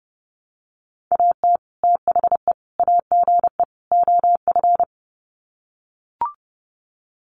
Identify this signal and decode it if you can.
Morse code, heard as a keyed tone, about 30 wpm, 700 Hz. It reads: AT THE AGE OF
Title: Sets of 4 Words 30wpm